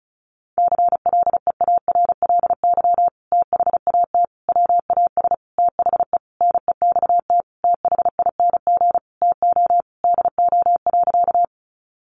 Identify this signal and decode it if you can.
Morse code, heard as a keyed tone, about 34 wpm, 700 Hz. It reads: CLEARLY THAT WAS THE NEXT THING TO DO.
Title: Sentences from Top 1300 Words 35wpm